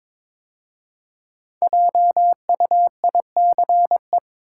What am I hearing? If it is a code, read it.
Morse code, heard as a keyed tone, about 21 wpm, 690 Hz. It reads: JUICE
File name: 5 Letter Words - First 3 Letters Given - 22wpm